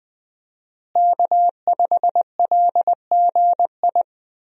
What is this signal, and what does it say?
Morse code, heard as a keyed tone, about 20 wpm, 700 Hz. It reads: K5LGI
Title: USA Call Signs - 5 Characters - Repeated 3x - 20wpm